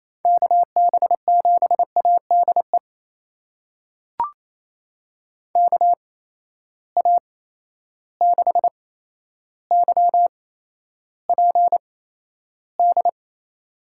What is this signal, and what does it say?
Morse code, heard as a keyed tone, about 28 wpm, spaced as for 12 wpm, 700 Hz. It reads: KB7ADE KA6YPD
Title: USA Call Signs - 6 Characters - 8x Character Spacing - 28wpm